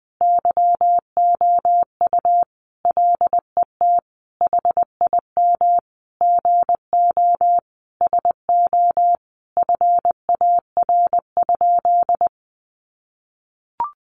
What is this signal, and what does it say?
Morse code, heard as a keyed tone, about 20 wpm, 700 Hz. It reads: YOU LET HIM GO SO FAR?
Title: Sentences from Top 300 Words 20wpm